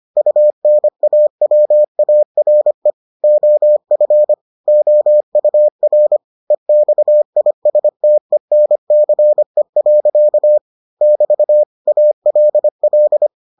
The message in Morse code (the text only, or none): UNAWARE OF OUR EXISTENCE. <BT> ALL